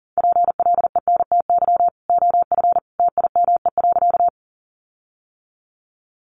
Morse code, heard as a keyed tone, about 40 wpm, 700 Hz. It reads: PLENTY OF TIME.
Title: Sentences from Top 1300 Words 40wpm